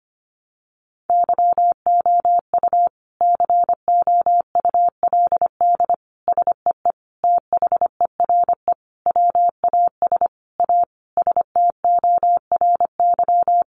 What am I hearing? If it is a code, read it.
Morse code, heard as a keyed tone, about 25 wpm, 700 Hz. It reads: YOU COULD SEE THERE WAS A STORY
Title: Sentences from Top 500 Words 25wpm